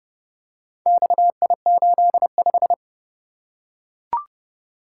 Morse code, heard as a keyed tone, about 29 wpm, 700 Hz. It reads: XI85